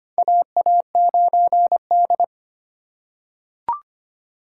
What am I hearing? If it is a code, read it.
Morse code, heard as a keyed tone, about 25 wpm, 700 Hz. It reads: AA9D
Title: USA Call Signs - 4 Characters - 4x Character Spacing - 25wpm